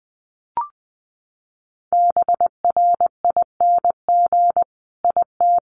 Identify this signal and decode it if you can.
Morse code, heard as a keyed tone, about 20 wpm, 700 Hz. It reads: BRING IT
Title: Sentences from Top 500 Words 20wpm